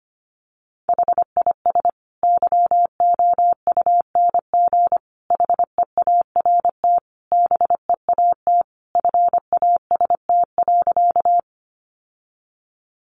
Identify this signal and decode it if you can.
Morse code, heard as a keyed tone, about 25 wpm, 700 Hz. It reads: HIS YOUNG HEART BEAT FAST.
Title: Sentences from Top 1300 Words 25wpm